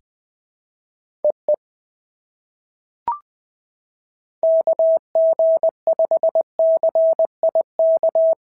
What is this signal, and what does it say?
Morse code, heard as a keyed tone, about 20 wpm, 690 Hz. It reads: EE KG5CIK
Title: POTA QSO Practice - 20wpm